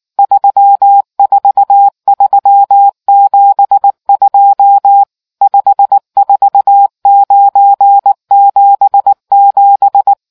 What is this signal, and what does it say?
Morse code, heard as a keyed tone, about 19 wpm, 800 Hz. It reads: 34372 54977